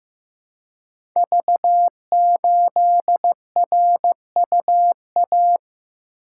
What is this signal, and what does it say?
Morse code, heard as a keyed tone, about 15 wpm, 700 Hz. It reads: V8RUA
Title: Code Groups - 5 Letter:Numbers - 15wpm